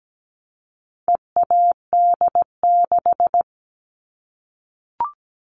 Morse code, heard as a keyed tone, about 17 wpm, 700 Hz. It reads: EAD6
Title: All Cut Numbers - 4-Digit 17wpm